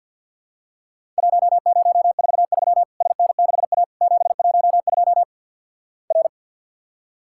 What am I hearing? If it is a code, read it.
Morse code, heard as a keyed tone, about 50 wpm, 700 Hz. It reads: 1043 SN6A 712 R
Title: Worked All Europe DX Contest - QTC Exchange - 50wpm